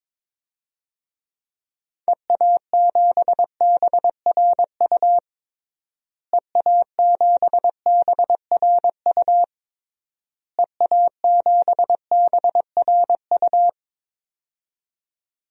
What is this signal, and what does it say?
Morse code, heard as a keyed tone, about 21 wpm, 700 Hz. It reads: EA7BRU EA7BRU EA7BRU
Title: DXCC-100 Call Signs - 6 Characters - Repeated 3x - 22wpm